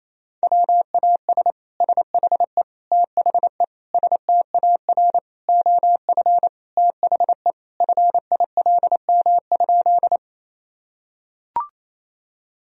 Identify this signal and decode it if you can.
Morse code, heard as a keyed tone, about 27 wpm, 700 Hz. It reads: WAS SHE THE STAR OF THE FILM?
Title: Sentences from Top 1000 Words - Review Part 3 - 28wpm